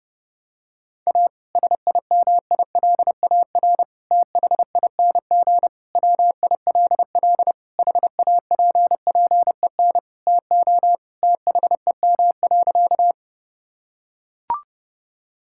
Morse code, heard as a keyed tone, about 30 wpm, 700 Hz. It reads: A SIMILAR THING WILL HAPPEN TO THEM.
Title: Sentences from Top 1000 Words - Review Part 5 - 30wpm